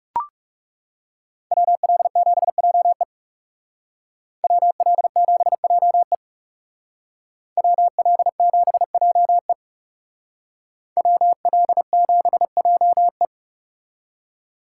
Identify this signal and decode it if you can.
Morse code, heard as a keyed tone, about 34 wpm, 700 Hz. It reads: WL7JE WL7JE WL7JE WL7JE
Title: Speed Racing - USA 2x2 Call Signs 30wpm